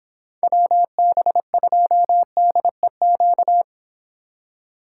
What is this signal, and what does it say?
Morse code, heard as a keyed tone, about 31 wpm, 700 Hz. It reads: WB2DEQ